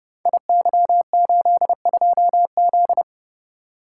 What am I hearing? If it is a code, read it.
Morse code, heard as a keyed tone, about 30 wpm, 700 Hz. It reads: IY82Z
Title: Code Groups - 5 Letter:Numbers - 30wpm